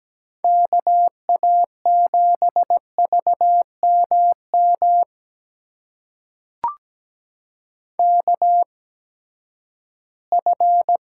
Morse code, heard as a keyed tone, about 17 wpm, spaced as for 12 wpm, 700 Hz. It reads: KA7VMM KF
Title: USA Call Signs - 6 Characters - 8x Character Spacing - 17wpm